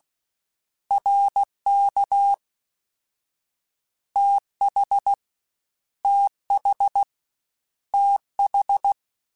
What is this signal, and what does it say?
Morse code, heard as a keyed tone, about 16 wpm, 800 Hz. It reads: RK TH TH TH
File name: Short Runs / Short Runs-mp3s / ShortContrasts @15